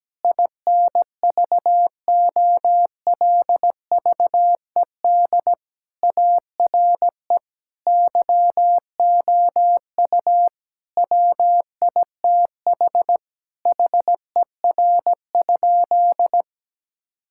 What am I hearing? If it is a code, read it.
Morse code, heard as a keyed tone, about 17 wpm, 700 Hz. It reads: INVOLVED ARE YOU WITH HER?
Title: Sentences from Top 1000 Words - Review Part 5 - 17wpm